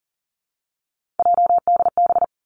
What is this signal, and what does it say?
Morse code, heard as a keyed tone, about 42 wpm, 700 Hz. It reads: JDB
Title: Three Challenging Letter-Number Combinations 40wpm